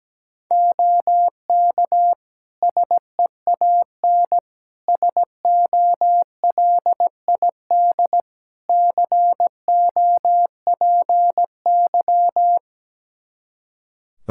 Morse code, heard as a keyed tone, about 17 wpm, 700 Hz. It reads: OK SEAN SOLID COPY
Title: Practice QSO - Exchange 3 - More Info - 17wpm